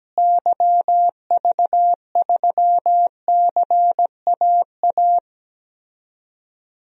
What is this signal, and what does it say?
Morse code, heard as a keyed tone, about 17 wpm, 700 Hz. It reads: YV3CAA
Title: DXCC-100 Call Signs - 6 Characters - Repeated 2x - 17wpm